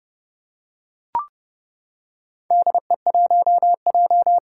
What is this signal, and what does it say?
Morse code, heard as a keyed tone, about 30 wpm, 700 Hz. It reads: DE1J
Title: Four Challenging Letter-Number Combinations 30wpm